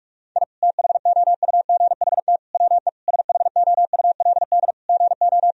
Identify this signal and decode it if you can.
Morse code, heard as a keyed tone, about 47 wpm, 700 Hz. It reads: I THOUGHT WE SHOULD GO
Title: Sentences from Top 1000 Words - Review Part 2 - 45wpm